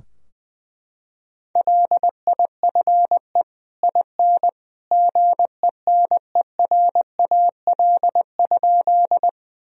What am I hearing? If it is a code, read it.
Morse code, heard as a keyed tone, about 20 wpm, 700 Hz. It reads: LIFE IN GENERAL?
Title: Sentences from Top 1000 Words - Review Part 3 - 20wpm